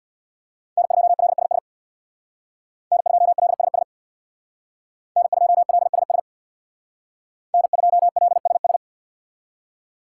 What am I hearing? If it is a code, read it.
Morse code, heard as a keyed tone, about 65 wpm, 700 Hz. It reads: N2LSS N2LSS N2LSS N2LSS